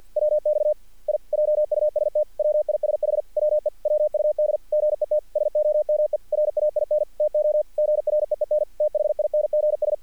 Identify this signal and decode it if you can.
Morse code, heard as a keyed tone, about 46 wpm, 600 Hz. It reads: WX I JUST WISH WE WUD GET SOME RAIN TO GREEN THINGS
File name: archived / CRSnet